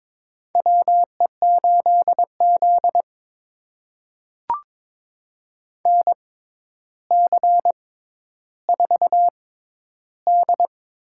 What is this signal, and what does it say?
Morse code, heard as a keyed tone, about 22 wpm, spaced as for 11 wpm, 700 Hz. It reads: WE8Z NC4D
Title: USA Call Signs - 6x Character Spacing - 22wpm